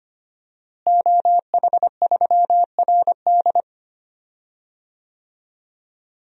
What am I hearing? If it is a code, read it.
Morse code, heard as a keyed tone, about 25 wpm, 700 Hz. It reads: OH3RD
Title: DXCC-100 Call Signs - 5 Characters - Repeated 3x - 25wpm